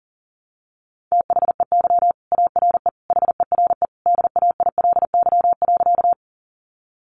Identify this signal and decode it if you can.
Morse code, heard as a keyed tone, about 40 wpm, 700 Hz. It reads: THEY ARE HERE DAILY.
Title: Sentences from Top 1100 Words 40wpm